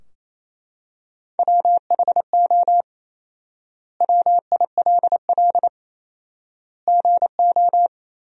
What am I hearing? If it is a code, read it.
Morse code, heard as a keyed tone, about 28 wpm, 700 Hz. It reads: WHO WILL GO
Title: 4x Word-Spacing - Sentences from Top 100 Words - 28wpm